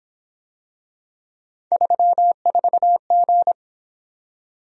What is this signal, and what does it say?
Morse code, heard as a keyed tone, about 27 wpm, 700 Hz. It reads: 34G
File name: Speed-Racing - Three Letter-Number Combinations - 22wpm